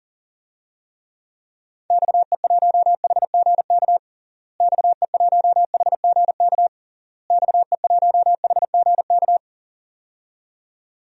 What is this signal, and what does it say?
Morse code, heard as a keyed tone, about 40 wpm, 700 Hz. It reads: XE1HGK XE1HGK XE1HGK
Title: DXCC-100 Call Signs - 6 Characters - Repeated 3x - 40wpm